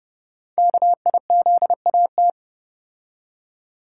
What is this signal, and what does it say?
Morse code, heard as a keyed tone, about 30 wpm, 700 Hz. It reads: KIZAT
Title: Code Groups - 5 Letters - 30wpm